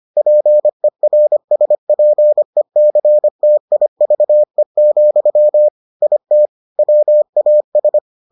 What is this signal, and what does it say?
Morse code, heard as a keyed tone, about 25 wpm, 600 Hz. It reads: PERSPECTIVE, IT WAS